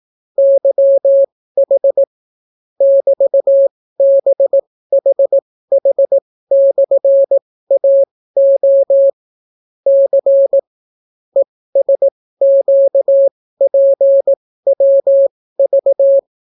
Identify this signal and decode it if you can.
Morse code, heard as a keyed tone, about 18 wpm, spaced as for 14 wpm, 550 Hz. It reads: YH <BT>BHH/AO C ESQPWV